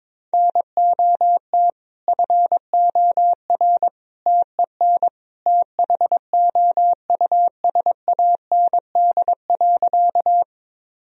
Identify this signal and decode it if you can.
Morse code, heard as a keyed tone, about 22 wpm, 700 Hz. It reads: NOT FOR TEN THOUSAND.
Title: Sentences from Top 1000 Words - Review Part 3 - 22wpm